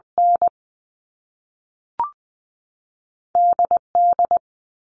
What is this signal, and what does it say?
Morse code, heard as a keyed tone, about 20 wpm, 700 Hz. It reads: N D D
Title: All Cut Numbers - 2-Digit 20wpm